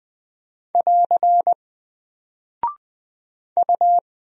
Morse code, heard as a keyed tone, about 20 wpm, 700 Hz. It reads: <AR> U
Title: Single Character (Letter:Number:Punctuation:Prosign) 20wpm